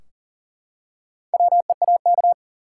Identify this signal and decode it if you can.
Morse code, heard as a keyed tone, about 40 wpm, 700 Hz. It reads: WEAK
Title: Four Letter Words - 40wpm